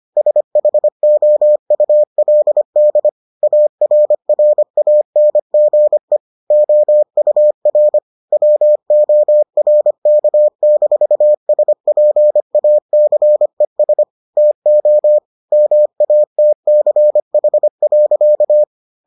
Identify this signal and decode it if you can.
Morse code, heard as a keyed tone, about 25 wpm, 600 Hz. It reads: SHOULD ARRANGE OUR WORK-SPACES TO MATCH.